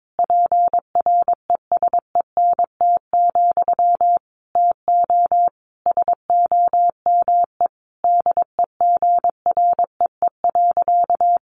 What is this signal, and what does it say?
Morse code, heard as a keyed tone, about 22 wpm, 700 Hz. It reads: PRESENT, TO SOME DEGREE.
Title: Sentences from Top 1500 Words 22wpm